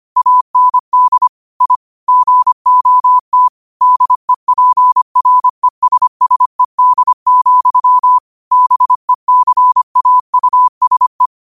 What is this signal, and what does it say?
Morse code, heard as a keyed tone, about 25 wpm, 1000 Hz. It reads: AND I GOT DEPRESSED, BECAUSE
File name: qotdfiles / 25wpm